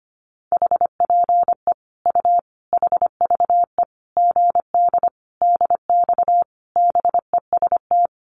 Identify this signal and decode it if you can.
Morse code, heard as a keyed tone, about 25 wpm, 700 Hz. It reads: HPE U HVE GD DX BEST
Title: Common QSO Phrases 25wpm